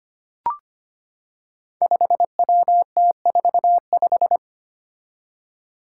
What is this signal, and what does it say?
Morse code, heard as a keyed tone, about 25 wpm, 700 Hz. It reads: 5WT45
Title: Five Challenging Letter-Number Combinations 25wpm